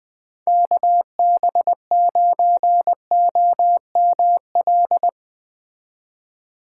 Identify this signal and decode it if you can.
Morse code, heard as a keyed tone, about 20 wpm, 700 Hz. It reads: KB9OML
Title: Speed Racing - USA 2x3 Call Signs 17wpm